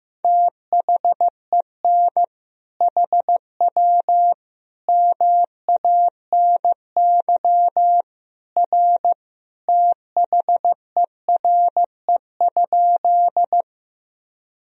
Morse code, heard as a keyed tone, about 15 wpm, 700 Hz. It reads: THEN HW MANY R THERE?
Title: Sentences from Top 100 Words and Abbreviations 15wpm